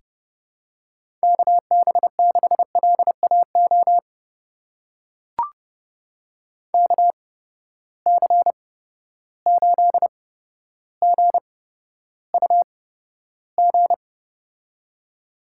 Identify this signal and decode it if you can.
Morse code, heard as a keyed tone, about 31 wpm, spaced as for 13 wpm, 700 Hz. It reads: KB6LAO KC8GUG
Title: USA Call Signs - 6 Characters - 8x Character Spacing - 30wpm